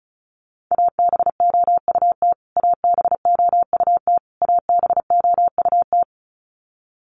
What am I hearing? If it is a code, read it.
Morse code, heard as a keyed tone, about 34 wpm, 700 Hz. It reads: ABOUT ABOUT ABOUT
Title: Top 200 Words Repeated 35wpm